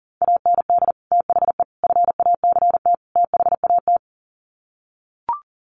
Morse code, heard as a keyed tone, about 40 wpm, 700 Hz. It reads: AND THE FACT THAT
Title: Sets of 4 Words 40wpm